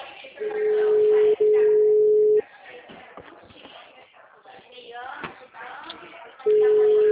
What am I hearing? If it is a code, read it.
Ringback tone (Europe, 425 Hz), with speech.